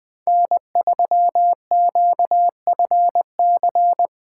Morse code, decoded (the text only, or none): N3QFC